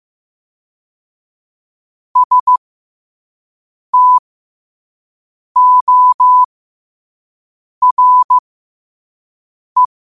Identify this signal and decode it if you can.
Morse code, heard as a keyed tone, about 15 wpm, spaced as for 5 wpm, 1000 Hz. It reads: STORE
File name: qotdfiles / 5wpm